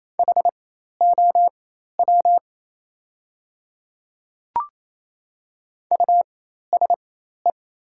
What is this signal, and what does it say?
Morse code, heard as a keyed tone, about 28 wpm, spaced as for 13 wpm, 700 Hz. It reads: HOW USE
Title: 4x Character Spacing - Top 100 Words - 28wpm